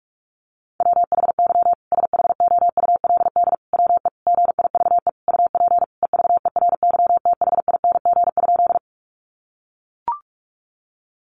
Morse code, heard as a keyed tone, about 44 wpm, 700 Hz. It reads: WHY SHOULD WE GIVE UP EVERYTHING?
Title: Sentences from Top 700 Words 45wpm